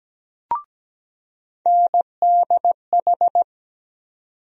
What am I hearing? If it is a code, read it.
Morse code, heard as a keyed tone, about 17 wpm, 700 Hz. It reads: NDH
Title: USA Call Sign Postfixes - 3 Letters 17wpm